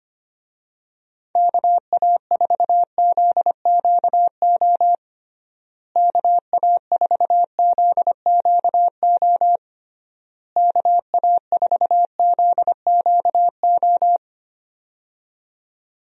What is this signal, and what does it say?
Morse code, heard as a keyed tone, about 25 wpm, 700 Hz. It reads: KA4ZQO KA4ZQO KA4ZQO
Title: USA Call Signs - 6 Characters - Repeated 3x - 25wpm